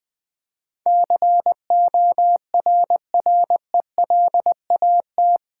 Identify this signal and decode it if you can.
Morse code, heard as a keyed tone, about 20 wpm, 700 Hz. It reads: CORRELAT
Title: Build A Word - 4 Letters At A Time - 11-Letter Words - 20wpm